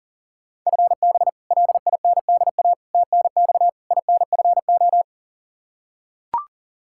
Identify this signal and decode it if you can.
Morse code, heard as a keyed tone, about 37 wpm, 700 Hz. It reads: FB LINDA TNX INFO
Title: Practice QSO - Exchange 3 - More Info - 40wpm